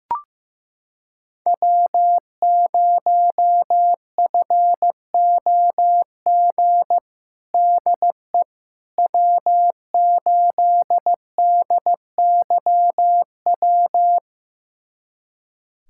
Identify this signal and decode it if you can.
Morse code, heard as a keyed tone, about 15 wpm, 700 Hz. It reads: W0FOG DE W8DYW
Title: USA Callsign DE Callsign - 5 Characters - 15wpm